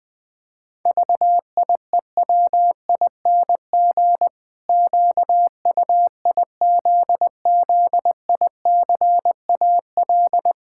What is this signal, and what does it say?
Morse code, heard as a keyed tone, about 21 wpm, 700 Hz. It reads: VIEWING QUIZZICAL
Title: Pangram Challenge v2 - 20wpm